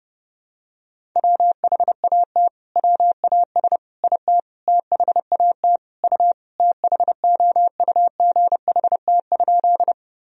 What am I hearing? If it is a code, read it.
Morse code, heard as a keyed tone, about 32 wpm, 700 Hz. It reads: WHAT WAS IT THAT U THOUGHT?
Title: Sentences from Top 300 Words and Abbreviations 30wpm